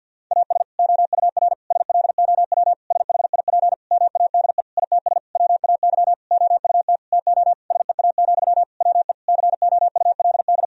Morse code, heard as a keyed tone, about 50 wpm, 700 Hz. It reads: AS OUR SLOW SHIP MADE ITS WAY OUT TO SEA, WE COULD
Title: Sentences from Top 1000 Words - Review Part 2 - 50wpm